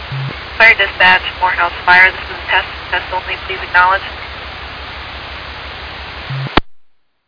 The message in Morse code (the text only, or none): none